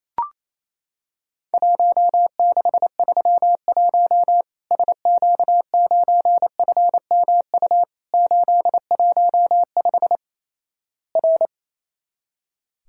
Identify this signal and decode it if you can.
Morse code, heard as a keyed tone, about 28 wpm, 700 Hz. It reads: 1631 SQ9FMU 815 R
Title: Worked All Europe DX Contest - QTC Exchange - 28wpm